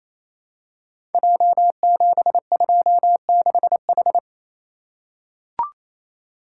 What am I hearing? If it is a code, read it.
Morse code, heard as a keyed tone, about 28 wpm, 700 Hz. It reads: J726H
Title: Code Groups - 5 Letter:Numbers - 28wpm